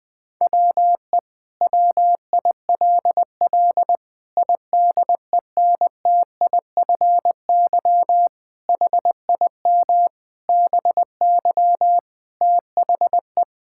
Morse code, heard as a keyed tone, about 20 wpm, 700 Hz. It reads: WE WILL IDENTIFY HIM BY THE
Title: Sentences from Top 1000 Words - Review Part 6 - 20wpm